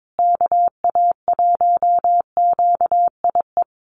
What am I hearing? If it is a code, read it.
Morse code, heard as a keyed tone, about 22 wpm, 700 Hz. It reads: KA1QIE
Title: USA Call Signs Repeated 3x 22wpm